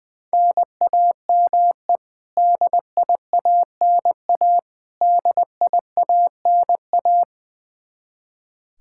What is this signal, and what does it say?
Morse code, heard as a keyed tone, about 20 wpm, 700 Hz. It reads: NAME DIANA DIANA
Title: Practice QSO - Exchange 1 - Beginning - 20wpm